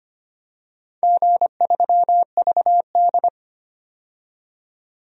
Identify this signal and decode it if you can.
Morse code, heard as a keyed tone, about 25 wpm, 700 Hz. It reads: G3VD
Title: DXCC-100 Call Signs - 4 Characters - 25wpm